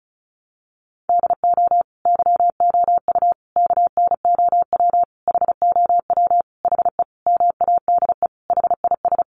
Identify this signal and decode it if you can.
Morse code, heard as a keyed tone, about 34 wpm, 700 Hz. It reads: DO YOU KNOW HOW HE MADE HIS
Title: Sentences from Top 300 Words 35wpm